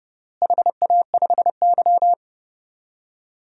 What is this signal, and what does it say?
Morse code, heard as a keyed tone, about 30 wpm, 700 Hz. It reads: HA5Y
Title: DXCC-100 Call Signs - 4 Characters - Repeated 3x - 30wpm